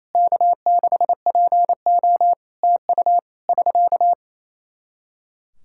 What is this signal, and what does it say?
Morse code, heard as a keyed tone, about 28 wpm, 700 Hz. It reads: K6PO TU <SK>
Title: Practice QSO - Exchange 4 - Ending - 28wpm